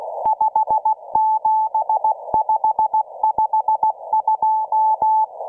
Morse code, heard as a keyed tone, about 16 wpm, 870 Hz. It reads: 57552